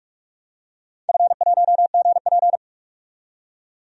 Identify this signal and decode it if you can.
Morse code, heard as a keyed tone, about 45 wpm, 700 Hz. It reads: F1GP
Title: DXCC-100 Call Signs - 4 Characters - 45wpm